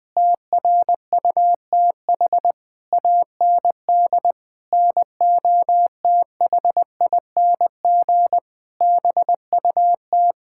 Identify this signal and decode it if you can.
Morse code, heard as a keyed tone, about 20 wpm, 700 Hz. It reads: TRUTH AND NOTHING BUT
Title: Sentences with 9-Word Phrases - 20wpm